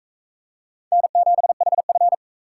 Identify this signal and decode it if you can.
Morse code, heard as a keyed tone, about 44 wpm, 700 Hz. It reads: N7HF